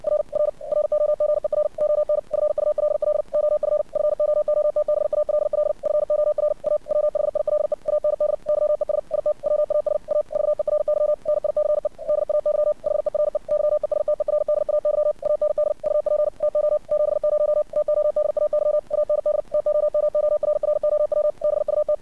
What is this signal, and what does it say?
Morse code, heard as a keyed tone, about 60 wpm, 600 Hz. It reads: R R ROGER ON FULL OF FOOTBALL FOR A WHILE AND YES IT WAS A VERY NICE DAY HERE YESTERDAY AND UP TO 70 TODAY AND TOMORROW BUT
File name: archived / CRSnet